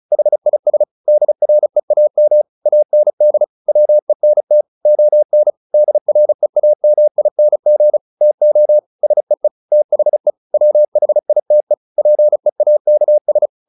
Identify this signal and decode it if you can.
Morse code, heard as a keyed tone, about 34 wpm, 600 Hz. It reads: HIS DREAM AND WENT ON DREAMING TO SEE THE WHITE PEAKS
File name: the old man and the sea / the old man and the sea 1 35wpm